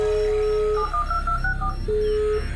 DTMF, heard as telephone touch tones, tones of about 90 ms. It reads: 712264